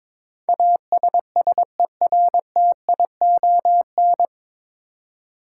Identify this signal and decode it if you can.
Morse code, heard as a keyed tone, about 23 wpm, 700 Hz. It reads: ASSERTION
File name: 9 Letter Words - First 3 Letters Given - 22wpm